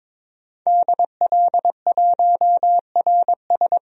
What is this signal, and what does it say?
Morse code, heard as a keyed tone, about 21 wpm, 700 Hz. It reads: DL1RS
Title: DXCC-100 Call Signs - 5 Characters - Repeated 3x - 22wpm